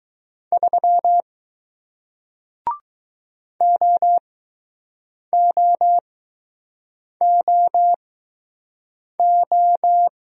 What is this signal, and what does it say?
Morse code, heard as a keyed tone, about 20 wpm, spaced as for 6 wpm, 700 Hz. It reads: 3 OOOO